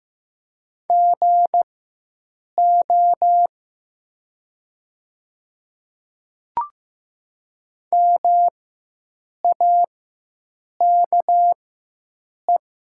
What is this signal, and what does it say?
Morse code, heard as a keyed tone, about 15 wpm, spaced as for 7 wpm, 700 Hz. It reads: GO MAKE
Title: 4x Character Spacing - Top 100 Words - 15wpm